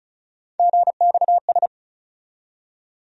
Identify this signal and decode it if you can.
Morse code, heard as a keyed tone, about 34 wpm, 700 Hz. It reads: GXS
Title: USA Call Sign Postfixes - 3 Letters 35wpm